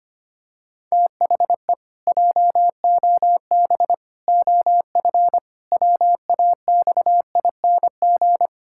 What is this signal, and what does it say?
Morse code, heard as a keyed tone, about 25 wpm, 700 Hz. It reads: THE JOB OF WAXING
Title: Pangram Challenge v2 - 25wpm